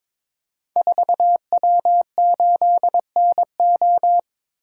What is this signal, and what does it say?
Morse code, heard as a keyed tone, about 22 wpm, 700 Hz. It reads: 4W8NO
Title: Code Groups - 5 Letter:Numbers - 22wpm